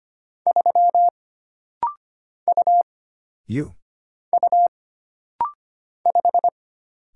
Morse code, heard as a keyed tone, about 25 wpm, 700 Hz. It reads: 3 U U 5
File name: Single Character (Letter:Number:Punctuation:Prosign) - Rapid-Fire - 25wpm